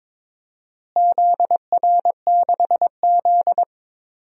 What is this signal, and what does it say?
Morse code, heard as a keyed tone, about 22 wpm, 700 Hz. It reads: ZR6Z
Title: DXCC-100 Call Signs - 4 Characters - Repeated 2x - 22wpm